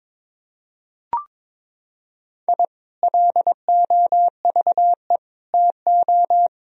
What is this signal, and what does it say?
Morse code, heard as a keyed tone, about 22 wpm, 700 Hz. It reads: I LOVE TO
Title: Sentences from Top 1000 Words - Review Part 3 - 22wpm